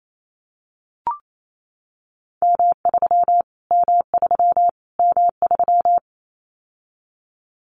Morse code, heard as a keyed tone, about 28 wpm, 700 Hz. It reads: M3 M3 M3
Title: Callsign Prefixes for Top 25 DXCC - Repeated 3x - 28wpm